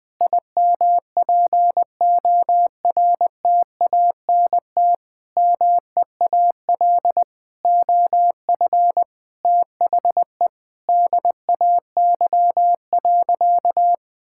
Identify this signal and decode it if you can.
Morse code, heard as a keyed tone, about 20 wpm, 700 Hz. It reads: IMPORTANT MEAL OF THE DAY.
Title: Sentences with 9-Word Phrases - Spoken Beginning - 20wpm